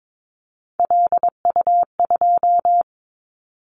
Morse code, heard as a keyed tone, about 22 wpm, 700 Hz. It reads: LU2
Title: Callsign Prefixes for Top 100 DXCC - 22wpm